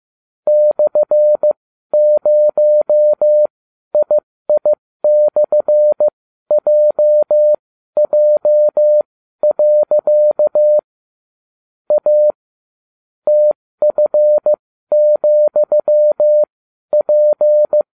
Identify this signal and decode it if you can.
Morse code, heard as a keyed tone, about 15 wpm, spaced as for 12 wpm, 600 Hz. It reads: /0II/JJ. A TF,P